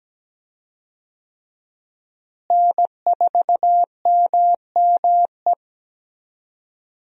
Morse code, heard as a keyed tone, about 17 wpm, 700 Hz. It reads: N4MME
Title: USA Call Signs - 5 Characters - Repeated 3x - 17wpm